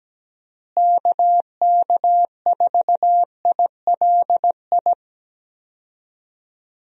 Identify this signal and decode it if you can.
Morse code, heard as a keyed tone, about 17 wpm, 700 Hz. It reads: KK4ILI
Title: USA Call Signs - 6 Characters - Repeated 3x - 17wpm